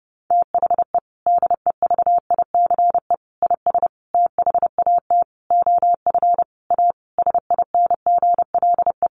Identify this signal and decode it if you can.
Morse code, heard as a keyed tone, about 30 wpm, 700 Hz. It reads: THE DEVICE IS THAT OF A SINGLE
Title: Sentences from Top 1500 Words 30wpm